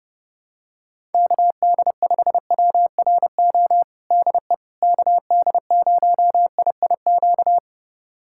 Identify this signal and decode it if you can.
Morse code, heard as a keyed tone, about 30 wpm, 700 Hz. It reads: KD5WRO DE KD0IIQ
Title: USA Callsign DE Callsign - 6 Characters - 30wpm